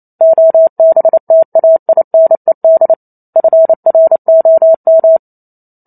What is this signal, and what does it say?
Morse code, heard as a keyed tone, about 28 wpm, 650 Hz. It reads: OBTAINED FROM